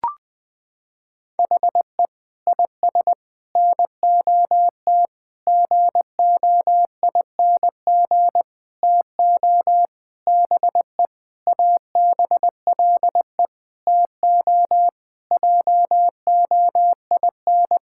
Morse code, heard as a keyed tone, about 19 wpm, 700 Hz. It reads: HE IS NOT GOING TO BE ABLE TO JOIN
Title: Sentences with 5-Word Phrases - 20wpm